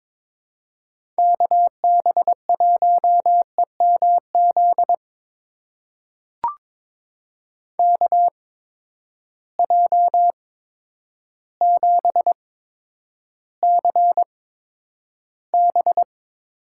Morse code, heard as a keyed tone, about 22 wpm, spaced as for 10 wpm, 700 Hz. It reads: KB1EMZ KJ7CB
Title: USA Call Signs - 6 Characters - 8x Character Spacing - 22wpm